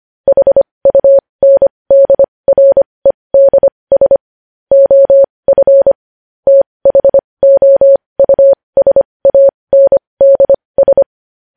Morse code, heard as a keyed tone, about 25 wpm, spaced as for 20 wpm, 550 Hz. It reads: HUNDREDS OF THOUSANDS
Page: News Headlines in Morse Code at 20 WPM